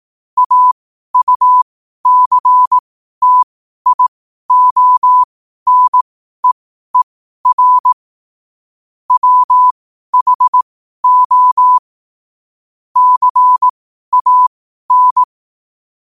Morse code, heard as a keyed tone, about 18 wpm, spaced as for 12 wpm, 1000 Hz. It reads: AUCTIONEER WHO CAN
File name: qotdfiles / 12wpm